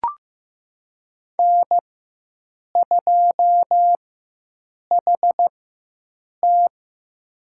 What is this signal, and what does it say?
Morse code, heard as a keyed tone, about 15 wpm, spaced as for 7 wpm, 700 Hz. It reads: N2HT
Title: USA Call Signs - 4 Characters - 4x Character Spacing - 15wpm